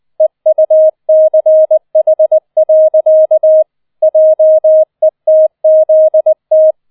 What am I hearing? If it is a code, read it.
Morse code, heard as a keyed tone, about 20 wpm, 630 Hz. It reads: EUCH. JETZT